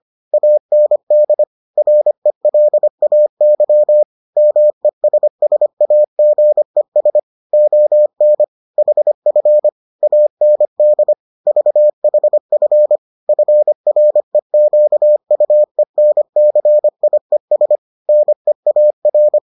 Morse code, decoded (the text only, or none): AND RELAY MESSAGES ON HF AND VHF FREQUENCIES NEAR